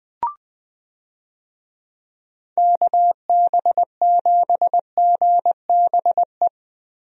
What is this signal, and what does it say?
Morse code, heard as a keyed tone, about 20 wpm, 700 Hz. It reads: KB7GBE
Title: USA Call Signs - 6 Characters - 20wpm